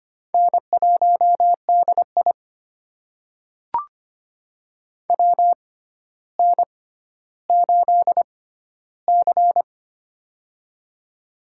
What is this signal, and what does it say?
Morse code, heard as a keyed tone, about 25 wpm, spaced as for 13 wpm, 700 Hz. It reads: N1DI WN8C
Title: USA Call Signs - 6x Character Spacing - 25wpm